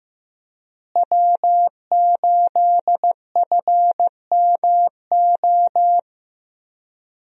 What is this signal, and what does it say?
Morse code, heard as a keyed tone, about 15 wpm, 700 Hz. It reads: W8FMO
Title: USA Call Signs - 5 Characters - Repeated 2x - 15wpm